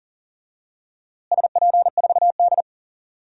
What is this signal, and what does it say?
Morse code, heard as a keyed tone, about 37 wpm, 700 Hz. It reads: SP4D
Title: DXCC-100 Call Signs - 4 Characters - Repeated 3x - 40wpm